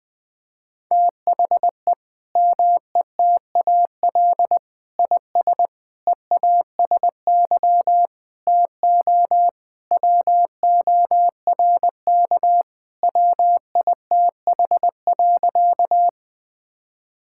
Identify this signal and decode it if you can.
Morse code, heard as a keyed tone, about 21 wpm, 700 Hz. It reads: THE METAL IS EASY TO WORK WITH.
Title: Sentences from Top 1000 Words - Review Part 6 - 20wpm